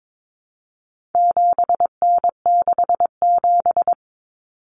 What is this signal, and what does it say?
Morse code, heard as a keyed tone, about 22 wpm, 700 Hz. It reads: 7N67
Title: Common Cut Numbers - 4-Digit 22wpm